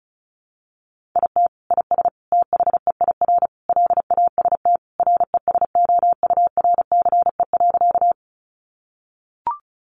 Morse code, heard as a keyed tone, about 34 wpm, 700 Hz. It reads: IT IS THEIR LAST RESOURCE.
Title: Sentences from Top 1300 Words 35wpm